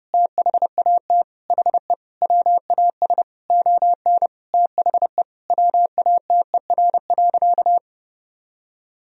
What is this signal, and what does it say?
Morse code, heard as a keyed tone, about 30 wpm, 700 Hz. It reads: THAT HE WAS ON THE WATER.
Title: Sentences from Top 500 Words and Abbreviations 30wpm